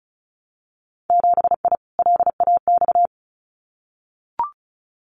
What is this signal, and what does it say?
Morse code, heard as a keyed tone, about 36 wpm, 700 Hz. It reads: 7I LAX
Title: Winter Field Day Exchanges Encore 35wpm